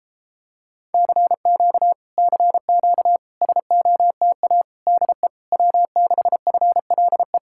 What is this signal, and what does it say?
Morse code, heard as a keyed tone, about 33 wpm, 700 Hz. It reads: CQ CQ SOTA DE W6FLE